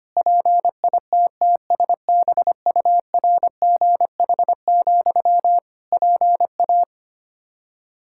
Morse code, heard as a keyed tone, about 25 wpm, 700 Hz. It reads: PITTSBURGH, PA
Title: Practice QSO - Exchange 1 - Beginning - 25wpm